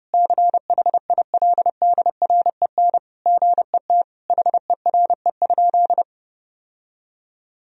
Morse code, heard as a keyed tone, about 30 wpm, 700 Hz. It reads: CHILDREN GET HERE?
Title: Sentences from Top 1000 Words - Review Part 1 - 30wpm